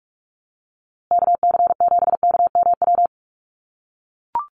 Morse code, heard as a keyed tone, about 44 wpm, 700 Hz. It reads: KC7KMW
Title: USA Call Signs Repeated 3x 45wpm